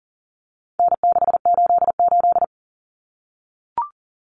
Morse code, heard as a keyed tone, about 40 wpm, 700 Hz. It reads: N688
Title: Common Cut Numbers - 4-Digit 40wpm